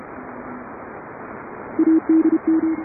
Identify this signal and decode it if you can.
Morse code, heard as a keyed tone, about 30 wpm, 310 Hz. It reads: ADM